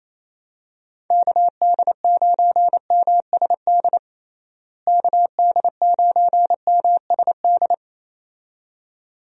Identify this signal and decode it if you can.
Morse code, heard as a keyed tone, about 28 wpm, 700 Hz. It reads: KD9MSD KD9MSD
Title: USA Call Signs - 6 Characters - Repeated 2x - 28wpm